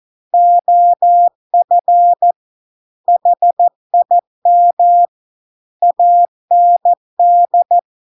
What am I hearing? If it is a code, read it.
Morse code, heard as a keyed tone, about 14 wpm, 700 Hz. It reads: OF HIM AND